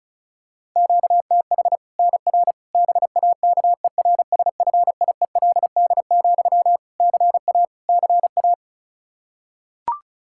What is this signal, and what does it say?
Morse code, heard as a keyed tone, about 36 wpm, 700 Hz. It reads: QTH NR BAKERSFIELD, CA CA
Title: Practice QSO - Exchange 1 - Beginning - 35wpm